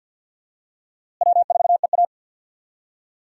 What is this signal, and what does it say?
Morse code, heard as a keyed tone, about 49 wpm, 700 Hz. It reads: W4EA
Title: USA Call Signs - 4 Characters - Repeated 3x - 50wpm